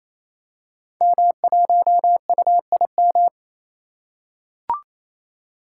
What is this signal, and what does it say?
Morse code, heard as a keyed tone, about 28 wpm, 700 Hz. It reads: M1UIM
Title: Code Groups - 5 Letter:Numbers - 28wpm